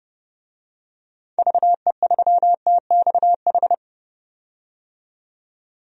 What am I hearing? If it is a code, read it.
Morse code, heard as a keyed tone, about 30 wpm, 700 Hz. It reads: VE3TXH
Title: DXCC-100 Call Signs - 6 Characters - 30wpm